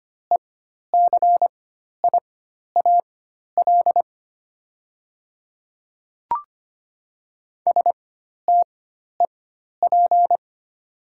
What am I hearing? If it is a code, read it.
Morse code, heard as a keyed tone, about 25 wpm, spaced as for 12 wpm, 700 Hz. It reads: ECIAL STEP
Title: 4x Character Spacing - Top 500 Words - 25wpm